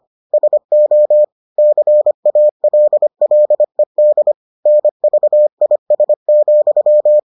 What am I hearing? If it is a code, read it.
Morse code, heard as a keyed tone, about 25 wpm, 600 Hz. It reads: SO CALLED NVIS,